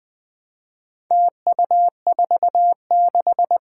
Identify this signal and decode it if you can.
Morse code, heard as a keyed tone, about 20 wpm, 700 Hz. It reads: TU46
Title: Four Challenging Letter-Number Combinations 20wpm